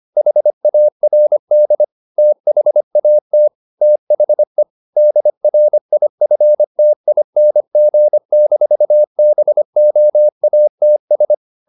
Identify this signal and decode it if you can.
Morse code, heard as a keyed tone, about 25 wpm, 600 Hz. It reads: HARD THAT THE DRIFTING-BOATS